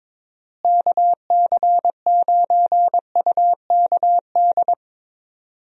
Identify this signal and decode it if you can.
Morse code, heard as a keyed tone, about 22 wpm, 700 Hz. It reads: KC9UKD
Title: USA Call Signs - 6 Characters - Repeated 2x - 22wpm